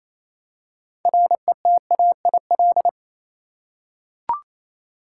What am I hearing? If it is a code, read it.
Morse code, heard as a keyed tone, about 27 wpm, 700 Hz. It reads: RETAIL
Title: Practice QSO - Exchange 3 - More Info - 28wpm